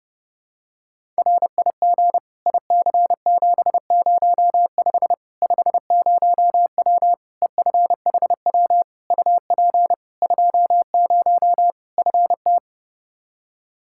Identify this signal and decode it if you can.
Morse code, heard as a keyed tone, about 32 wpm, 700 Hz. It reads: RIG IC705 50W EFHW UP 20 FT
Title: Practice QSO - Exchange 2 - Info - 30wpm